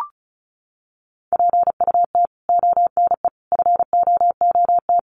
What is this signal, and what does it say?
Morse code, heard as a keyed tone, about 34 wpm, 700 Hz. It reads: PUT ONE FOOT
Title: Sentences from Top 1100 Words 35wpm